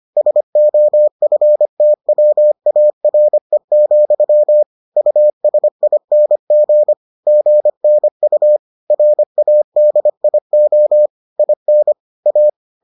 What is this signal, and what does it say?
Morse code, heard as a keyed tone, about 25 wpm, 600 Hz. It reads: SOFTWARE, USING GNU RADIO IN A